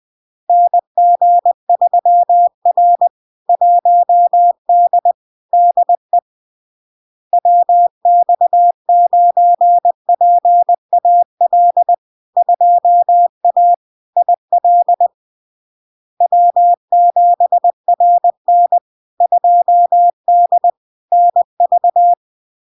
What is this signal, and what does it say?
Morse code, heard as a keyed tone, about 20 wpm, 700 Hz. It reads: NG3R 1D DE WX9PAL 2A IL W7RN 2D NV
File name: LARK-FD-2022-Log-in-CW-20wpm-700-Hz